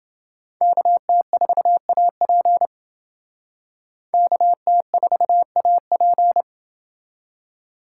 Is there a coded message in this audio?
Yes: Morse code from 0.6 to 6.4 s, heard as a keyed tone, about 28 wpm, 700 Hz. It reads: KT4AP KT4AP